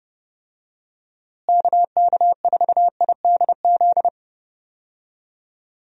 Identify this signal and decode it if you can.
Morse code, heard as a keyed tone, about 30 wpm, 700 Hz. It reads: KK4IDZ
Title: USA Call Signs - 6 Characters - Repeated 2x - 30wpm